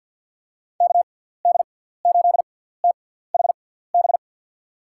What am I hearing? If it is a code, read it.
Morse code, heard as a keyed tone, about 53 wpm, spaced as for 17 wpm, 700 Hz. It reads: KD8THB
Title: USA Call Signs - 6x Character Spacing - 50wpm